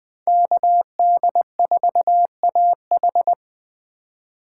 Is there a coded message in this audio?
Yes: Morse code from 0.3 to 3.3 s, heard as a keyed tone, about 20 wpm, 700 Hz. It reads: KD4AH